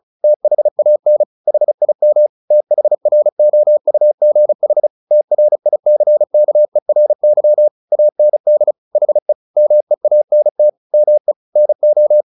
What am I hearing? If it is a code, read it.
Morse code, heard as a keyed tone, about 34 wpm, 600 Hz. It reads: THAN HIM THROUGH TRICKERY AND HE MEANT ME NO
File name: the old man and the sea 2 35wpm